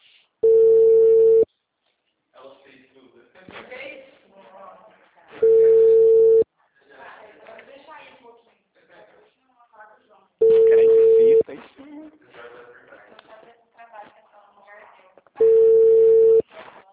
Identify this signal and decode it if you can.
Ringback tone (China, 450 Hz), with speech.
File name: out- -2005- - - .10358